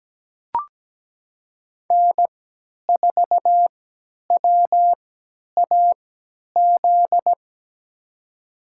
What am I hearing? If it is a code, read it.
Morse code, heard as a keyed tone, about 17 wpm, spaced as for 10 wpm, 700 Hz. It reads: N4WAZ
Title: USA Call Signs - 5 Characters - 3x Character Spacing - 17wpm